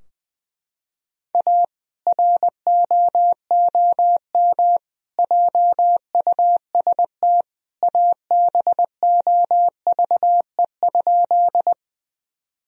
Morse code, heard as a keyed tone, about 21 wpm, 700 Hz. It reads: A ROOM JUST ABOVE?
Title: Sentences from Top 1000 Words - Review Part 2 - 20wpm